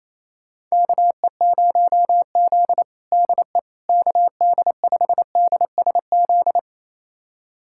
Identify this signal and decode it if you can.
Morse code, heard as a keyed tone, about 28 wpm, 700 Hz. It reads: KE0Z DE KD5DSZ
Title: USA Callsign DE Callsign 28wpm